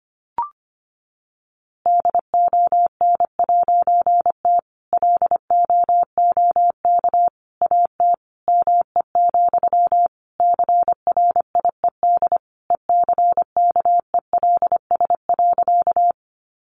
Morse code, heard as a keyed tone, about 25 wpm, 700 Hz. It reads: DON'T LOOK AT ME, CRIED ECKELS.
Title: A Sound Of Thunder 25wpm